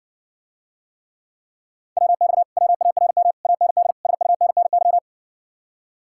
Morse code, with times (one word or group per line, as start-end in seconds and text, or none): 2.0-2.4 WX
2.6-3.3 WARM
3.5-3.9 AND
4.1-5.0 SUNNY